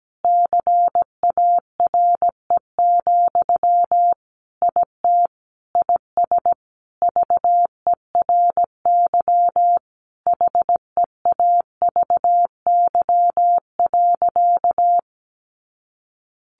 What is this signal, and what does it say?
Morse code, heard as a keyed tone, about 17 wpm, 700 Hz. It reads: CARE, IT IS VERY HEAVY.
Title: Sentences from Top 1100 Words 17wpm